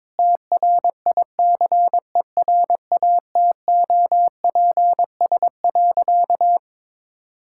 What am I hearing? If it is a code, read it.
Morse code, heard as a keyed tone, about 22 wpm, 700 Hz. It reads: TRICERATOPS.
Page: morse-code-ninja-practice-22wpm